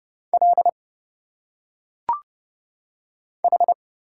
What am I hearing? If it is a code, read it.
Morse code, heard as a keyed tone, about 29 wpm, 700 Hz. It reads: L H
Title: Single Letters - Phonetic Pronunciation 30wpm